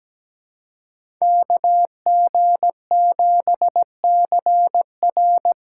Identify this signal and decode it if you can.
Morse code, heard as a keyed tone, about 20 wpm, 700 Hz. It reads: KG7CR
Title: Speed Racing - USA 2x2 Call Signs 17wpm